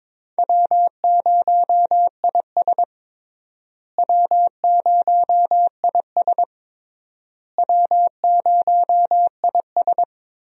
Morse code, heard as a keyed tone, about 22 wpm, 700 Hz. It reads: W0IS W0IS W0IS
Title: USA Call Signs - 4 Characters - Repeated 3x - 22wpm